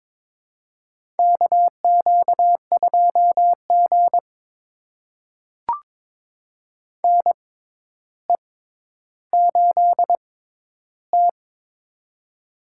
Morse code, heard as a keyed tone, about 21 wpm, spaced as for 11 wpm, 700 Hz. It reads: KQ2G NE8T
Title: USA Call Signs - 4 Characters - 6x Character Spacing - 22wpm